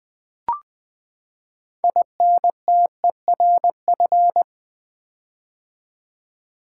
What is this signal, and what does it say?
Morse code, heard as a keyed tone, about 20 wpm, 700 Hz. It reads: INTERF
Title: Build A Word - 6 Letters At A Time - 12-Letter Words - 20wpm